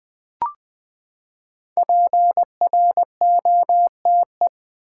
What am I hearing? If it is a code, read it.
Morse code, heard as a keyed tone, about 20 wpm, 700 Hz. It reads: PROTE
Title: Build A Word - 5 Letters At A Time - 10-Letter Words - 20wpm